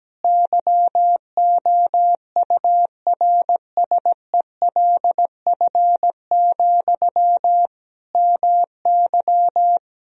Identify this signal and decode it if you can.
Morse code, heard as a keyed tone, about 17 wpm, 700 Hz. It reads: YOURSELF, MY
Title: Sentences from Top 1000 Words - Review Part 3 - 17wpm